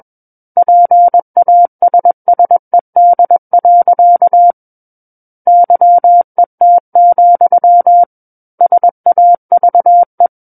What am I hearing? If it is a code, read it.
Morse code, heard as a keyed tone, about 21 wpm, 700 Hz. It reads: PASSED. YET, SAVE
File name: Book One / 12-Chapter 12 - 21 WPM